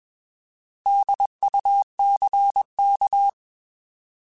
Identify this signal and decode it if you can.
Morse code, heard as a keyed tone, about 21 wpm, 800 Hz. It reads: DUCK
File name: Alphabetical / Alphabetical mp3s / Ds @20